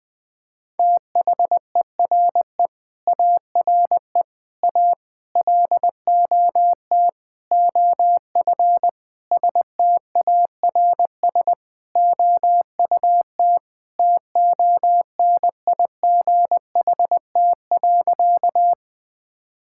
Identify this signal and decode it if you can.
Morse code, heard as a keyed tone, about 19 wpm, 700 Hz. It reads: THERE ARE A LOT OF STARS OUT TONIGHT.
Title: Sentences with 5-Word Phrases - 20wpm